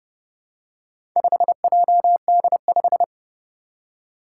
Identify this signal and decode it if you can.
Morse code, heard as a keyed tone, about 30 wpm, 700 Hz. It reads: 5JD5
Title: Four Challenging Letter-Number Combinations 30wpm